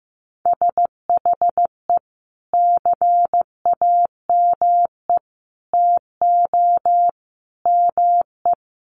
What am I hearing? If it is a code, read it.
Morse code, heard as a keyed tone, about 15 wpm, 700 Hz. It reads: SHE CAME TO ME
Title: Sentences from Top 1100 Words 15wpm